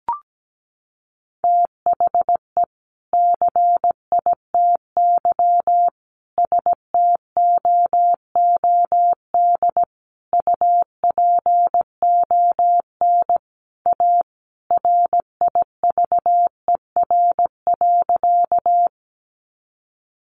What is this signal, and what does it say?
Morse code, heard as a keyed tone, about 17 wpm, 700 Hz. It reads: THE CITY STOOD UPON A RIVER.
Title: Sentences from Top 1100 Words 17wpm